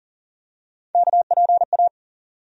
Morse code, heard as a keyed tone, about 40 wpm, 700 Hz. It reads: KPA
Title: USA Call Sign Postfixes - 3 Letters 40wpm